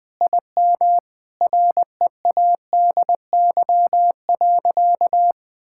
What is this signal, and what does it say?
Morse code, heard as a keyed tone, about 21 wpm, 700 Hz. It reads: IM READY.
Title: Sentences with 5-Word Phrases - 20wpm